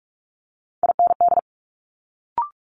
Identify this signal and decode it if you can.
Morse code, heard as a keyed tone, about 46 wpm, 700 Hz. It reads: IND
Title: Multi-letter Phonemes Quick Fire 45wpm